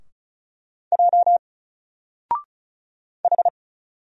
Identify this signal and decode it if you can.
Morse code, heard as a keyed tone, about 33 wpm, 700 Hz. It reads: J H
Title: Single Letter-Number Alt 35wpm